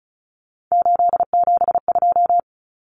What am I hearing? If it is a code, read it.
Morse code, heard as a keyed tone, about 34 wpm, 700 Hz. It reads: 872
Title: Common Cut Numbers - 3-Digit 35wpm